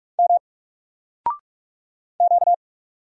Morse code, heard as a keyed tone, about 44 wpm, 700 Hz. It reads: M Q
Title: Single Character (Letter:Number:Punctuation:Prosign) 45wpm